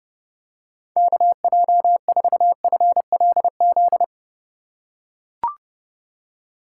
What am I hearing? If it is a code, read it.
Morse code, heard as a keyed tone, about 31 wpm, 700 Hz. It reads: KJ4FLZ
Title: USA Call Signs - 6 Characters - 2x Character Spacing - 30wpm